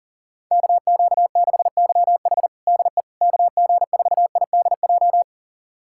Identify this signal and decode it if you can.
Morse code, heard as a keyed tone, about 37 wpm, 700 Hz. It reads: KQ6YH DE KG4IDJ
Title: USA Callsign DE Callsign 40wpm